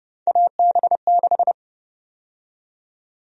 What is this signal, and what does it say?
Morse code, heard as a keyed tone, about 32 wpm, 700 Hz. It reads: AB6
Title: USA Call Sign Prefixes - 30wpm